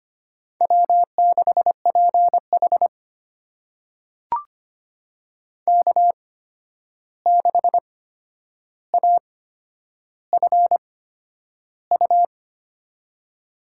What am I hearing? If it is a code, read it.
Morse code, heard as a keyed tone, about 25 wpm, spaced as for 10 wpm, 700 Hz. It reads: W6PH K6AFU
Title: USA Call Signs - 8x Character Spacing - 25wpm